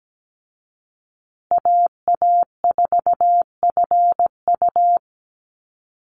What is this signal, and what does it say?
Morse code, heard as a keyed tone, about 17 wpm, 700 Hz. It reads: AA4FU
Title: Call Signs Repeated 2x - 17wpm